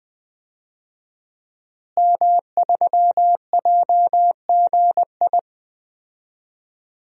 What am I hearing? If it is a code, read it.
Morse code, heard as a keyed tone, about 20 wpm, 700 Hz. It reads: M3JGI
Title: DXCC-100 Call Signs - 5 Characters - Repeated 2x - 20wpm